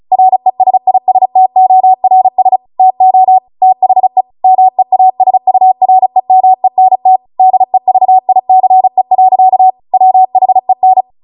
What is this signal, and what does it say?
Morse code, heard as a keyed tone, about 36 wpm, 750 Hz. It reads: RESISTORS TO THE MEASUREMENT DEVICE. WHEN